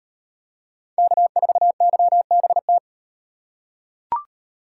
Morse code, heard as a keyed tone, about 38 wpm, 700 Hz. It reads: K4YBT